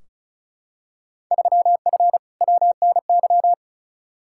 Morse code, heard as a keyed tone, about 36 wpm, 700 Hz. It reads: 3F WNY
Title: Field Day Exchanges - 35wpm